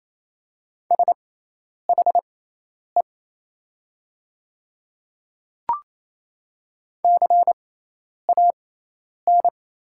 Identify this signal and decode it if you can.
Morse code, heard as a keyed tone, about 28 wpm, spaced as for 10 wpm, 700 Hz. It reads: SHE CAN
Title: 6x Character Spacing - Top 100 Words - 28wpm